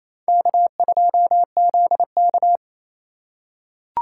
Morse code, heard as a keyed tone, about 28 wpm, 700 Hz. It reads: K2ZK
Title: USA Call Signs - 4x Character Spacing - 28wpm